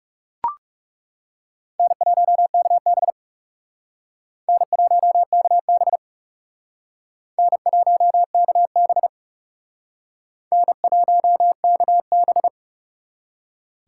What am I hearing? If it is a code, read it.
Morse code, heard as a keyed tone, about 37 wpm, 700 Hz. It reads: N1KB N1KB N1KB N1KB
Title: Speed Racing - USA 1x2 Call Signs 30wpm